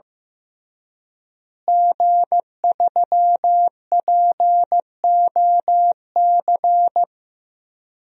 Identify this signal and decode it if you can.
Morse code, heard as a keyed tone, about 15 wpm, 700 Hz. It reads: G3POC
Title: DXCC-100 Call Signs - 5 Characters - Repeated 3x - 15wpm